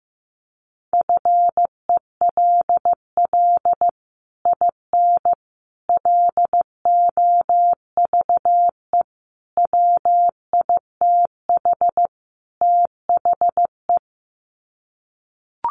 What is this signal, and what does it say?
Morse code, heard as a keyed tone, about 15 wpm, 700 Hz. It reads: FELL IN LOVE WITH THE
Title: Sets of 5 Words Encore2 15wpm